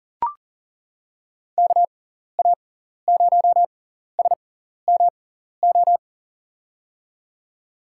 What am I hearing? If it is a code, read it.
Morse code, heard as a keyed tone, about 39 wpm, spaced as for 14 wpm, 700 Hz. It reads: KA0SMO
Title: USA Call Signs - 6 Characters - 6x Character Spacing - 40wpm